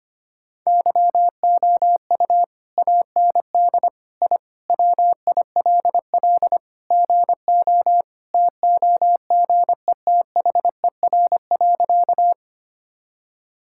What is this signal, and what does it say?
Morse code, heard as a keyed tone, about 25 wpm, 700 Hz. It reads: YOU AND I WILL GO TOGETHER.
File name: practice / 25wpm / Sentences with 2-Word Phrases - 25wpm